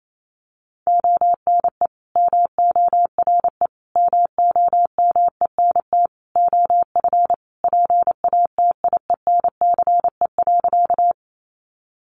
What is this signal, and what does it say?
Morse code, heard as a keyed tone, about 28 wpm, 700 Hz. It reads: ONE MORE MOMENT OF PATIENCE.
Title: Sentences from Top 1300 Words 28wpm